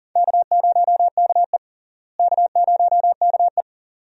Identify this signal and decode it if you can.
Morse code, heard as a keyed tone, about 40 wpm, 700 Hz. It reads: K0KE K0KE
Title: USA Call Signs - 4 Characters - Repeated 2x - 40wpm